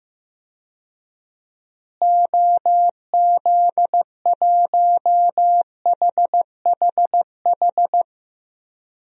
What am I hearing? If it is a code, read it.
Morse code, heard as a keyed tone, about 15 wpm, 700 Hz. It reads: OZ1HHH
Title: DXCC-100 Call Signs - 6 Characters - Repeated 2x - 15wpm